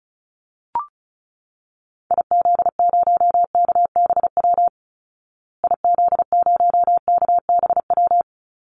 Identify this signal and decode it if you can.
Morse code, heard as a keyed tone, about 34 wpm, 700 Hz. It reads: IZ0KBW IZ0KBW
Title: Call Signs Repeated 2x - 35wpm